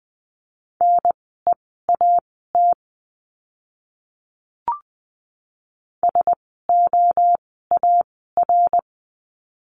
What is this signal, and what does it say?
Morse code, heard as a keyed tone, about 20 wpm, spaced as for 14 wpm, 700 Hz. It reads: NEAT SOAR
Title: 054 Lesson 10f - Letter r - Words 20wpm